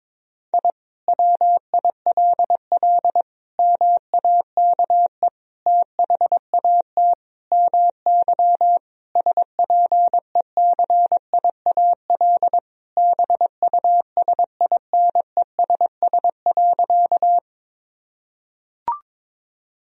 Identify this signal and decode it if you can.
Morse code, heard as a keyed tone, about 22 wpm, 700 Hz. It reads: I WILL MAKE THAT MY SPECIAL BUSINESS.
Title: Sentences from Top 1000 Words - Review Part 3 - 22wpm